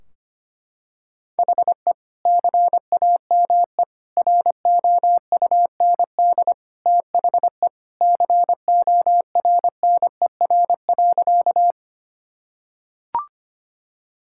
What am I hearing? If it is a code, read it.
Morse code, heard as a keyed tone, about 25 wpm, 700 Hz. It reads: HE CAME ROUND THE CORNER.
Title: Sentences from Top 1600 Words 25wpm